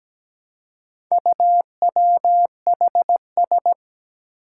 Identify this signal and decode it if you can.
Morse code, heard as a keyed tone, about 17 wpm, 700 Hz. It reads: UWHS
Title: Four Challenging Letter-Number Combinations 17wpm